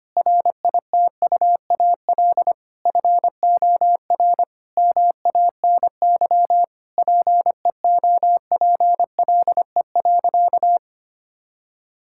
Morse code, heard as a keyed tone, about 25 wpm, 700 Hz. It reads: RITUAL FOR MANY PEOPLE.
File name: practice / 25wpm / Difficult Sentences with Hints - 25wpm